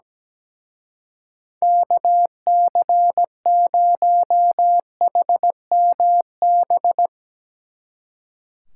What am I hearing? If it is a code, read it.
Morse code, heard as a keyed tone, about 19 wpm, 700 Hz. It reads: KC0HMB
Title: Speed Racing - USA 2x3 Call Signs 17wpm